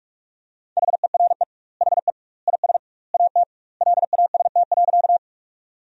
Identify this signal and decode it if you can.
Morse code, heard as a keyed tone, about 46 wpm, 700 Hz. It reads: HERE HE IS AT LAST.
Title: 2x Word-Spacing - Sentences from Top 300 Words - 45wpm